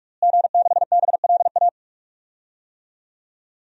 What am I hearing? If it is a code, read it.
Morse code, heard as a keyed tone, about 46 wpm, 700 Hz. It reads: G6BLA
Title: DXCC-100 Call Signs - 5 Characters - Repeated 2x - 45wpm